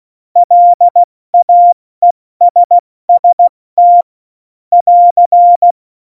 Morse code, heard as a keyed tone, about 16 wpm, 700 Hz. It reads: LAESST <AR>
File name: Morsekurs / Witze / Witz 02. Januar 2018, 70 F80